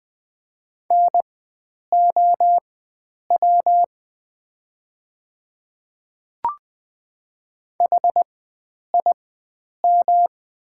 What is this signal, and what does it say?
Morse code, heard as a keyed tone, about 20 wpm, spaced as for 9 wpm, 700 Hz. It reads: NOW HIM
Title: 4x Character Spacing - Top 100 Words - 20wpm